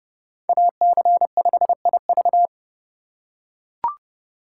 Morse code, heard as a keyed tone, about 31 wpm, 700 Hz. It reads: AC5IV
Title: USA Call Signs - 5 Characters - 4x Character Spacing - 30wpm